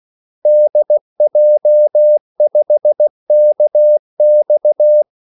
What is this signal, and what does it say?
Morse code, heard as a keyed tone, about 16 wpm, 600 Hz. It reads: DJ5KX
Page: Kaldesignaler 16 wpm